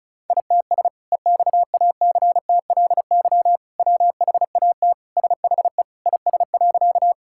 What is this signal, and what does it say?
Morse code, heard as a keyed tone, about 36 wpm, 690 Hz. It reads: ITS EXACTLY WHAT SHE IS.
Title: Sentences from Top 1000 Words - Review Part 6 - 35wpm